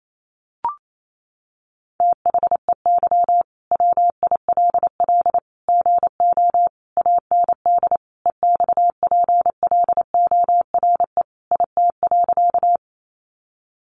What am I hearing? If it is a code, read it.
Morse code, heard as a keyed tone, about 28 wpm, 700 Hz. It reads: THEY WILL GO AND EXPLORE IT.
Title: Sentences from Top 1100 Words 28wpm